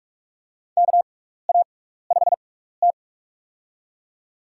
Morse code, heard as a keyed tone, about 42 wpm, spaced as for 15 wpm, 700 Hz. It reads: KA5T
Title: USA Call Signs - 4 Characters - 6x Character Spacing - 45wpm